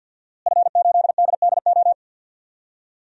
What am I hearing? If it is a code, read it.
Morse code, heard as a keyed tone, about 47 wpm, 700 Hz. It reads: F8DDO
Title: DXCC-100 Call Signs - 5 Characters - Repeated 3x - 50wpm